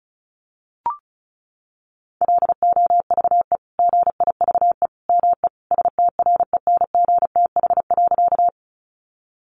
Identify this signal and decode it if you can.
Morse code, heard as a keyed tone, about 34 wpm, 700 Hz. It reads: LOVE GIVE ME STRENGTH.
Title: Sentences from Top 1300 Words 35wpm